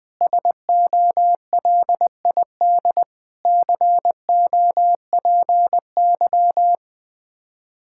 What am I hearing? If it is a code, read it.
Morse code, heard as a keyed tone, about 20 wpm, 700 Hz. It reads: SOLID COPY
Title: Practice QSO - Exchange 3 - More Info - 20wpm